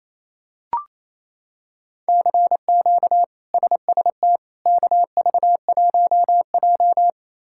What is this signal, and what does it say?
Morse code, heard as a keyed tone, about 28 wpm, 700 Hz. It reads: CQ SST KV1J
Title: K1USN SST QSO Practice - 28wpm